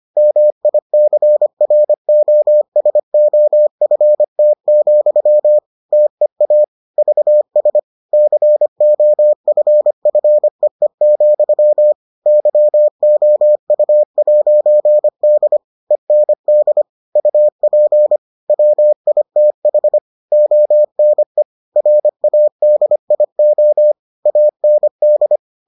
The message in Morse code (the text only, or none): MICROSOFT, TEA VS COFFEE, YOU'D END UP WITH ONE RADIO AND